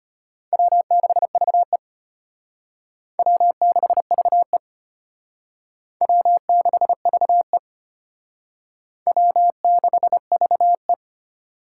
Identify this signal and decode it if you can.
Morse code, heard as a keyed tone, about 33 wpm, 700 Hz. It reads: W6VE W6VE W6VE W6VE